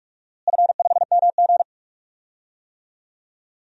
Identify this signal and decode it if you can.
Morse code, heard as a keyed tone, about 44 wpm, 700 Hz. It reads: F5MG
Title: DXCC-100 Call Signs - 4 Characters - Repeated 2x - 45wpm